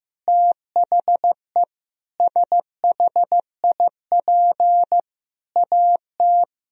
Morse code, heard as a keyed tone, about 15 wpm, 700 Hz. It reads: THE SHIP AT
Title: Sentences from Top 1000 Words - Review Part 2 - 15wpm